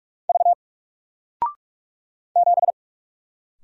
Morse code, heard as a keyed tone, about 44 wpm, 700 Hz. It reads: V 7
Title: Single Character (Letter:Number:Punctuation:Prosign) 45wpm